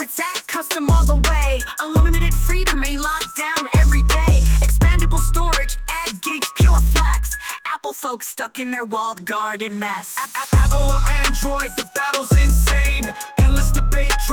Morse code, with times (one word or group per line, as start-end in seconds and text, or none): none